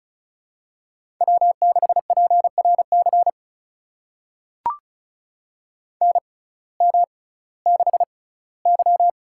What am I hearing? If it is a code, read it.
Morse code, heard as a keyed tone, about 36 wpm, spaced as for 19 wpm, 700 Hz. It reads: W6PRC NM6Y
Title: USA Call Signs - 6x Character Spacing - 35wpm